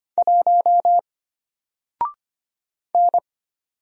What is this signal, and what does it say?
Morse code, heard as a keyed tone, about 25 wpm, 700 Hz. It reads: 1 N
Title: Single Letter-Number 25wpm